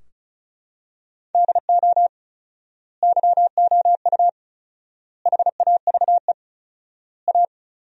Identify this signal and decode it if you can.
Morse code, heard as a keyed tone, about 34 wpm, 700 Hz. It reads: DO YOU HAVE A
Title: 4x Word-Spacing - Sentences from Top 300 Words - 35wpm